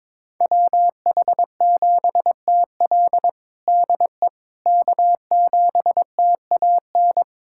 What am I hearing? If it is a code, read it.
Morse code, heard as a keyed tone, about 22 wpm, 700 Hz. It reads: WH7TL DE K7TAN
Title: USA Callsign DE Callsign - 5 Characters - 22wpm